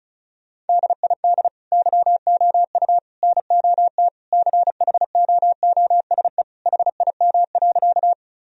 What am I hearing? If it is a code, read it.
Morse code, heard as a keyed tone, about 36 wpm, 700 Hz. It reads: DID YOU NOT CHOOSE HIM.
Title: Sentences from Top 1000 Words - Review Part 4 - 35wpm